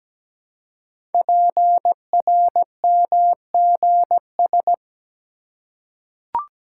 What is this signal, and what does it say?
Morse code, heard as a keyed tone, about 17 wpm, 700 Hz. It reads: PRMGS
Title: Code Groups - 5 Letters - 17wpm